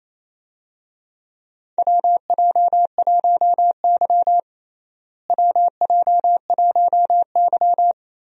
Morse code, heard as a keyed tone, about 28 wpm, 700 Hz. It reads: WJ1Y WJ1Y
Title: USA Call Signs - 4 Characters - Repeated 3x - 28wpm